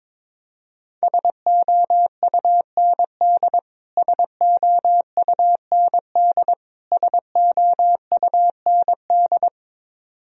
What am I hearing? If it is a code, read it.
Morse code, heard as a keyed tone, about 22 wpm, 700 Hz. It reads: SOUND SOUND SOUND
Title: Top 100 Words Repeated 22wpm